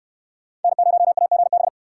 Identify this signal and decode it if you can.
Morse code, heard as a keyed tone, about 65 wpm, 700 Hz. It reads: N0AGB